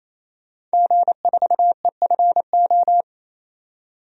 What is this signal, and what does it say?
Morse code, heard as a keyed tone, about 28 wpm, 700 Hz. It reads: G4EFO
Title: DXCC-100 Call Signs - 5 Characters - 28wpm